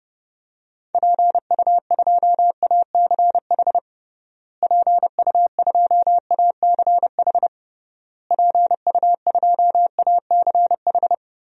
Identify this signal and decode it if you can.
Morse code, heard as a keyed tone, about 30 wpm, 700 Hz. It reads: PU2ACH PU2ACH PU2ACH
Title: DXCC-100 Call Signs - 6 Characters - Repeated 3x - 30wpm